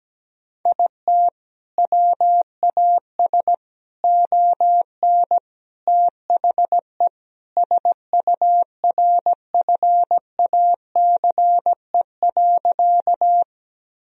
Morse code, heard as a keyed tone, about 17 wpm, 700 Hz. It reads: IT WAS ON THE SURFACE.
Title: Sentences from Top 1000 Words - Review Part 3 - 17wpm